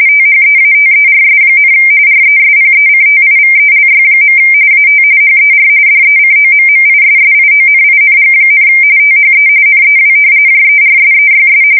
RTTY, about 45.45 baud. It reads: THEATERS, RESPECTIVELY. AT HER DECOMMISSIONING, THE NEW JERSEYS
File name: fsk170 / battleshipnj